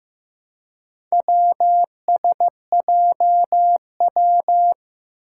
Morse code, heard as a keyed tone, about 15 wpm, 700 Hz. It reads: WSJW
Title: Four Challenging Letter-Number Combinations 15wpm